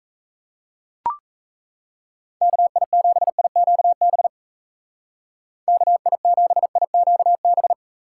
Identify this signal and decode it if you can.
Morse code, heard as a keyed tone, about 34 wpm, 700 Hz. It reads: KI7IQB KI7IQB
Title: Speed Racing - USA 2x3 Call Signs 28wpm